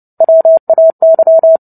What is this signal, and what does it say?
Morse code, heard as a keyed tone, about 30 wpm, 650 Hz. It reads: WAY
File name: Book Two / 03-Chapter 3 - 29 WPM